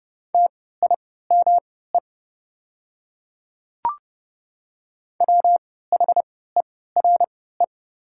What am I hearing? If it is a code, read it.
Morse code, heard as a keyed tone, about 29 wpm, spaced as for 17 wpm, 700 Hz. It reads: TIME WHERE
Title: 3x Character Spacing - Top 200 Words - 30wpm